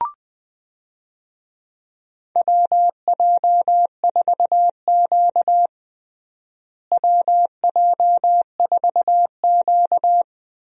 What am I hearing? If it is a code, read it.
Morse code, heard as a keyed tone, about 20 wpm, 700 Hz. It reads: WJ4Q WJ4Q
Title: USA Call Signs - 4 Characters - Repeated 3x - 20wpm